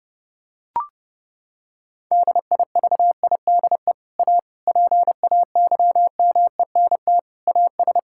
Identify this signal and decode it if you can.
Morse code, heard as a keyed tone, about 30 wpm, 700 Hz. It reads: DIVIDE A PAYMENT AS
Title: Sentences from Top 1600 Words 30wpm